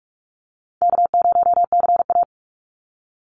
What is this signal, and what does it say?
Morse code, heard as a keyed tone, about 46 wpm, 700 Hz. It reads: K0CA
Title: Easy Call Signs - 45wpm